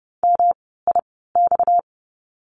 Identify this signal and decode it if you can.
Morse code, heard as a keyed tone, about 30 wpm, spaced as for 17 wpm, 700 Hz. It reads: MIX
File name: course / 30:10wpm / 230 Lesson 37f - Letter x - Words 30wpm